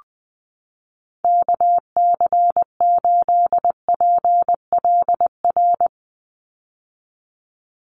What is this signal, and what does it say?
Morse code, heard as a keyed tone, about 20 wpm, 700 Hz. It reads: KC8PLR
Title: USA Call Signs 20wpm